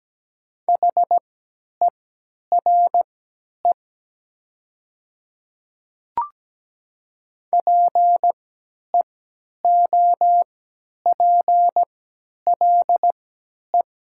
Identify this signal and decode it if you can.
Morse code, heard as a keyed tone, about 17 wpm, spaced as for 10 wpm, 690 Hz. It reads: HERE PEOPLE
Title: 3x Character Spacing - Top 200 Words - 17wpm